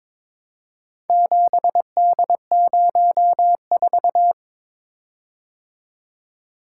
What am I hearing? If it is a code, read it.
Morse code, heard as a keyed tone, about 21 wpm, 700 Hz. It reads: 7D04
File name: practice / 22wpm / Four Letter-Number Combinations - 22wpm